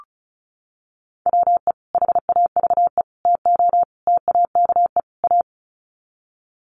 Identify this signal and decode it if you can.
Morse code, heard as a keyed tone, about 34 wpm, 700 Hz. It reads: WE HAVE TO TAKE A
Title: Sets of 5 Words Encore 35wpm